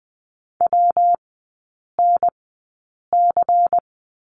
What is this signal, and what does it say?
Morse code, heard as a keyed tone, about 20 wpm, spaced as for 8 wpm, 700 Hz. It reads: WNC
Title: 114 Lesson 19e - Letter w - Three Characters 20wpm